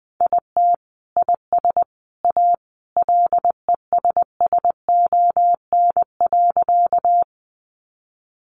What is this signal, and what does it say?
Morse code, heard as a keyed tone, about 20 wpm, 700 Hz. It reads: IT IS A LESSON.
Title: Sentences from Top 1200 Words 20wpm